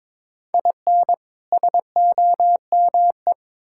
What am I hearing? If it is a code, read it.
Morse code, heard as a keyed tone, about 22 wpm, 700 Hz. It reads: IN SOME
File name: Build A Phrase - 2-Word Phrases - 22wpm